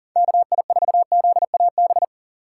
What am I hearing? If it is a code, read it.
Morse code, heard as a keyed tone, about 50 wpm, 700 Hz. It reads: KI4ZAB
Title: Speed Racing - USA 2x3 Call Signs 40wpm